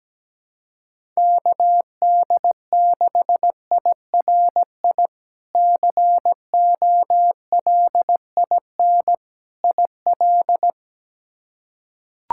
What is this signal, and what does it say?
Morse code, heard as a keyed tone, about 17 wpm, 700 Hz. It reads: KD6IRI COLIN IL
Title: K1USN SST QSO Practice - 17wpm